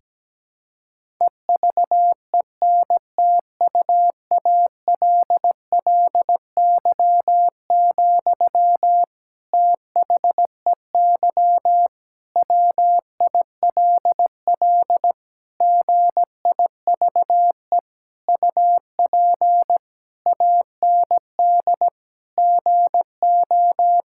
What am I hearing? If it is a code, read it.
Morse code, heard as a keyed tone, about 17 wpm, 700 Hz. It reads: EVENTUALLY, THEY WILL GIVE UP AND GO
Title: Sentences from Top 1000 Words - Review Part 6 - 17wpm